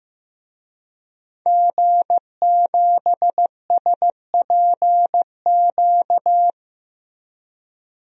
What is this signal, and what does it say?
Morse code, heard as a keyed tone, about 15 wpm, 700 Hz. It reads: G7SPQ
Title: DXCC-100 Call Signs - 5 Characters - Repeated 2x - 15wpm